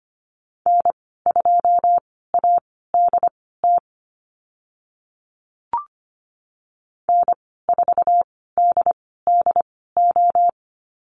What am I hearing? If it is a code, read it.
Morse code, heard as a keyed tone, about 25 wpm, spaced as for 16 wpm, 700 Hz. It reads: N2ADT N4DDO
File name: Lesson 15g - Number 5 - Call Signs 25wpm